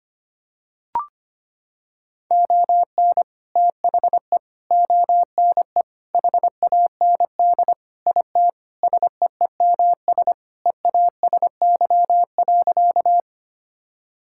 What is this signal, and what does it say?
Morse code, heard as a keyed tone, about 25 wpm, 700 Hz. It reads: ON THE ONE HAND IT SEEMS EASY.
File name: practice / 25wpm / Sentences with 4-Word Phrases - 25wpm